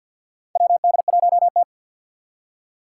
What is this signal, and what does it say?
Morse code, heard as a keyed tone, about 46 wpm, 700 Hz. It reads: WD1T
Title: Four Challenging Letter-Number Combinations 50wpm